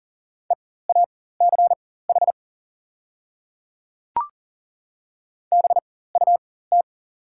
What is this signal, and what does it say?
Morse code, heard as a keyed tone, about 39 wpm, spaced as for 18 wpm, 690 Hz. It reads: EACH BUT
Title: 4x Character Spacing - Top 200 Words - 40wpm